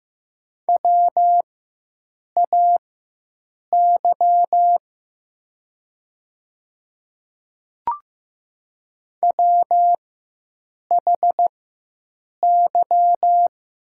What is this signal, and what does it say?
Morse code, heard as a keyed tone, about 15 wpm, spaced as for 7 wpm, 700 Hz. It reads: WAY WHY
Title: 4x Character Spacing - Top 500 Words - 15wpm